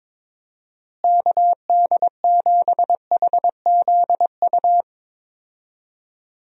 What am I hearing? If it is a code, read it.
Morse code, heard as a keyed tone, about 22 wpm, 700 Hz. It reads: KD7HZU
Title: USA Call Signs - 6 Characters - Repeated 3x - 22wpm